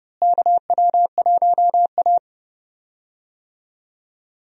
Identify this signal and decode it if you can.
Morse code, heard as a keyed tone, about 32 wpm, 700 Hz. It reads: KW1A
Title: USA Call Signs - 4 Characters - Repeated 2x - 30wpm